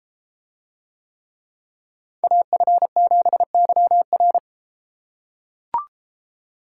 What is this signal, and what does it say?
Morse code, heard as a keyed tone, about 31 wpm, 700 Hz. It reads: AF7YR